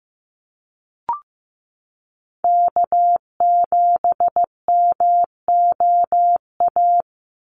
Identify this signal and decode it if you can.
Morse code, heard as a keyed tone, about 15 wpm, 700 Hz. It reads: K7MOA
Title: Call Signs Repeated 2x - 15wpm